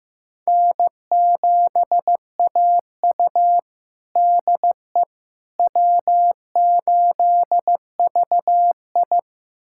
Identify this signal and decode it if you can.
Morse code, heard as a keyed tone, about 16 wpm, 700 Hz. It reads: N7AU DE W8VI
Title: USA Callsign DE Callsign - 4 Characters - 15wpm